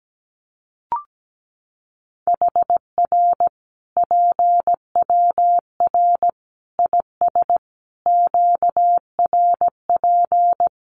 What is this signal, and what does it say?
Morse code, heard as a keyed tone, about 17 wpm, 700 Hz. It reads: HR PWR IS QRP
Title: Common QSO Phrases 17wpm